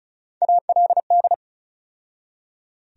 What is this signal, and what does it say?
Morse code, heard as a keyed tone, about 36 wpm, 700 Hz. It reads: ALD
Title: USA Call Sign Postfixes - 3 Letters 35wpm